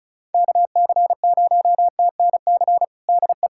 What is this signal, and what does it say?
Morse code, heard as a keyed tone, about 36 wpm, 700 Hz. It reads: KC0TNC DE
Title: USA Callsign DE Callsign - 6 Characters - 35wpm